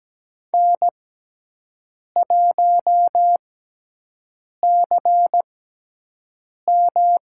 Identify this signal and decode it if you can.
Morse code, heard as a keyed tone, about 17 wpm, spaced as for 6 wpm, 700 Hz. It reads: N1CM
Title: USA Call Signs - 6x Character Spacing - 17wpm